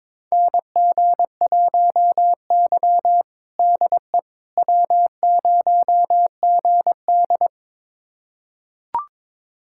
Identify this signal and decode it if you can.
Morse code, heard as a keyed tone, about 22 wpm, 700 Hz. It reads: NG1Y DE W0GD
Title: USA Callsign DE Callsign - 4 Characters - 22wpm